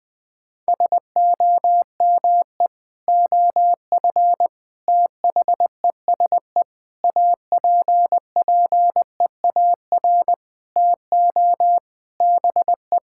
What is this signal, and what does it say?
Morse code, heard as a keyed tone, about 20 wpm, 700 Hz. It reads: SOME OF THESE APPEAR TO BE
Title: Sentences from Top 1000 Words - Review Part 6 - 20wpm